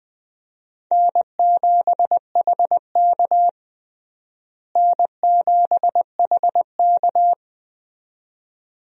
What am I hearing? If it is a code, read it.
Morse code, heard as a keyed tone, about 20 wpm, 700 Hz. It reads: N7HK N7HK
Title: USA Call Signs - 4 Characters - Repeated 2x - 20wpm